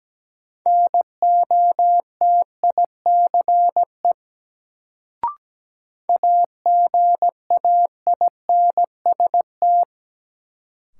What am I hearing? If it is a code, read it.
Morse code, heard as a keyed tone, about 17 wpm, 700 Hz. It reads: NOTICE AGAINST
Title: Sentences from Top 400 Words - New Word Drill - 17wpm